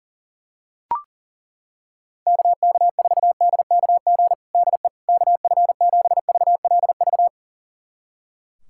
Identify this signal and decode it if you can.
Morse code, heard as a keyed tone, about 40 wpm, 700 Hz. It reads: KK4DKG DE KF7VLV
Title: USA Callsign DE Callsign 40wpm